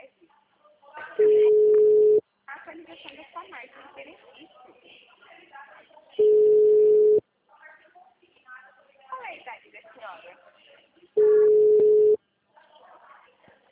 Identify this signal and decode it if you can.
Ringback tone (Europe, 425 Hz), with speech.
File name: out- -2004- - - .4519